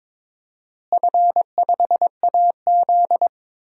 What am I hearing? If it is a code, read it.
Morse code, heard as a keyed tone, about 21 wpm, 700 Hz. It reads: F5AZ
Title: DXCC-100 Call Signs - 4 Characters - Repeated 2x - 22wpm